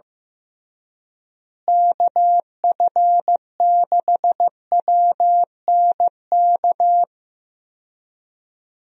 Morse code, heard as a keyed tone, about 15 wpm, 700 Hz. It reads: KF6WNK
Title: USA Call Signs - 6 Characters - Repeated 3x - 15wpm